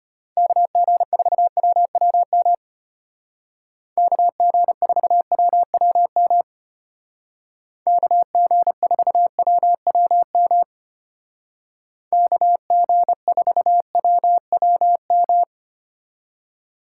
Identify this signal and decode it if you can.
Morse code, heard as a keyed tone, about 33 wpm, 700 Hz. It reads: KG4WWM KG4WWM KG4WWM KG4WWM